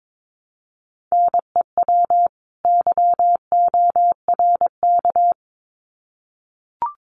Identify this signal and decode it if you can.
Morse code, heard as a keyed tone, about 22 wpm, 700 Hz. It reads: NEW YORK
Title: Sets of 2 Words 22wpm